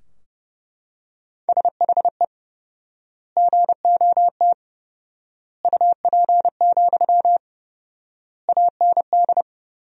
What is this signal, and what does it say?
Morse code, heard as a keyed tone, about 30 wpm, 700 Hz. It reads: SHE GOT UP, AND
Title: 4x Word-Spacing - Sentences from Top 500 Words - 30wpm